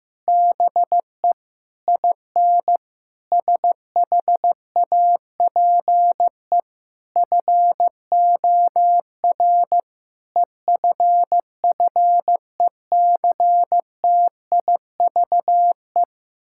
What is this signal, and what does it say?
Morse code, heard as a keyed tone, about 16 wpm, 700 Hz. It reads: BE IN SHAPE FOR EFFECTIVE
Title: Sentences from Top 1000 Words - Review Part 6 - 15wpm